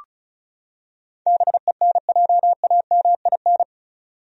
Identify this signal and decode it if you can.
Morse code, heard as a keyed tone, about 36 wpm, 700 Hz. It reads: BENJAMIN
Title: USA Common Names - 35wpm